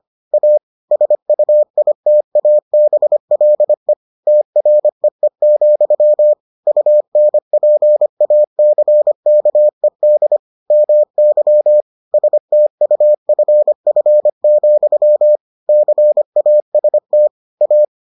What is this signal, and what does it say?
Morse code, heard as a keyed tone, about 25 wpm, 600 Hz. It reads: A SUITABLE TREE, UNPACKED MY STUFF, CAST A